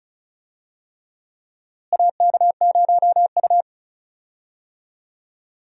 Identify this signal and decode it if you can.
Morse code, heard as a keyed tone, about 36 wpm, 700 Hz. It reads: AK0U
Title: USA Call Signs - 4 Characters - Repeated 3x - 35wpm